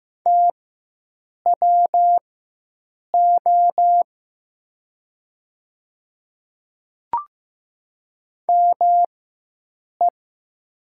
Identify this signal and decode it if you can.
Morse code, heard as a keyed tone, about 15 wpm, spaced as for 7 wpm, 700 Hz. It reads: TWO ME
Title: 4x Character Spacing - Top 300 Words - 15wpm